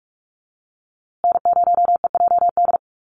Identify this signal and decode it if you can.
Morse code, heard as a keyed tone, about 44 wpm, 690 Hz. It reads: N0EJD
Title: USA Call Signs Repeated 2x 45wpm